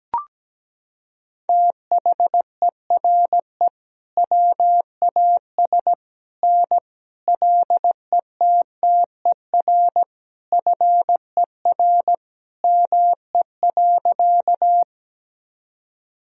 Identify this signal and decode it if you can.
Morse code, heard as a keyed tone, about 17 wpm, 700 Hz. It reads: THERE WAS N LETTER FER ME.
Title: Sentences from Top 300 Words and Abbreviations 17wpm